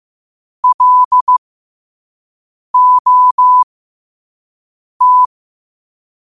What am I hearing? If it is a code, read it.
Morse code, heard as a keyed tone, about 15 wpm, spaced as for 5 wpm, 1000 Hz. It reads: LOT